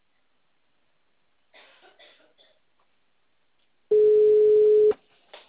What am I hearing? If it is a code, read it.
Ringback tone (Europe, 425 Hz).